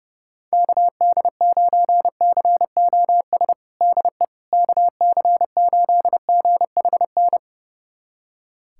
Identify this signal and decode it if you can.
Morse code, heard as a keyed tone, about 30 wpm, 700 Hz. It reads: KD9COS DE KC8GHN
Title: USA Callsign DE Callsign - 6 Characters - 30wpm